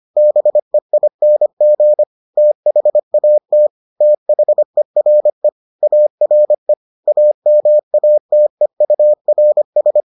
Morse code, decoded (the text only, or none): BEING THAT THERE ARE AMATEURS